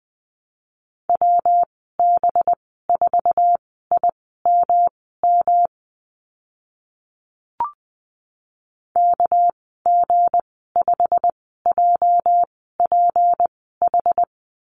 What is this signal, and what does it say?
Morse code, heard as a keyed tone, about 20 wpm, spaced as for 15 wpm, 700 Hz. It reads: WB4IMM KG5JPH
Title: 207 Lesson 33g - Letter j - Call Signs 20wpm